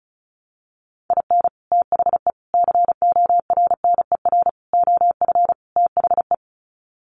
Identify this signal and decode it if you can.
Morse code, heard as a keyed tone, about 34 wpm, 700 Hz. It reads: IN THE CORNER OF THE
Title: Sets of 5 Words Encore 35wpm